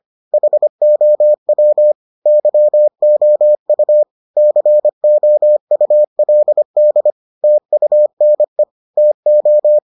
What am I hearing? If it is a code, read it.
Morse code, heard as a keyed tone, about 25 wpm, 600 Hz. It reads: HOW YOU COULD TUNE TO